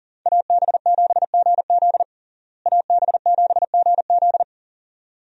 Morse code, heard as a keyed tone, about 37 wpm, 700 Hz. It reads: AB7GZ AB7GZ
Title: USA Call Signs - 5 Characters - Repeated 2x - 40wpm